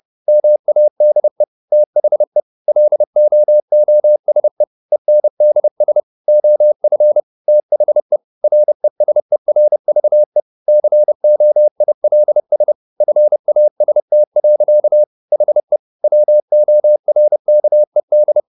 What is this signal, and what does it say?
Morse code, heard as a keyed tone, about 30 wpm, 600 Hz. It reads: MADE THE LOOSE ENDS OF THE RESERVE COILS FAST. HE WORKED
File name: the old man and the sea / the old man and the sea 1 30wpm